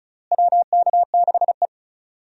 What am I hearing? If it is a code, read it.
Morse code, heard as a keyed tone, about 36 wpm, 700 Hz. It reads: WK6E
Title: USA Call Signs - 4 Characters - Repeated 3x - 35wpm